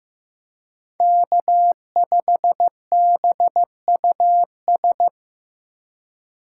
Morse code, heard as a keyed tone, about 18 wpm, 700 Hz. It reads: K5BUS
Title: Speed Racing - USA 1x3 Call Signs 15wpm